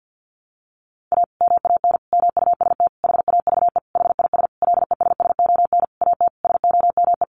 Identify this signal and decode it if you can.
Morse code, heard as a keyed tone, about 50 wpm, 700 Hz. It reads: A MAN MUST HAVE HIS LESSON AT SOME
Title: Sentences from Top 1200 Words 50wpm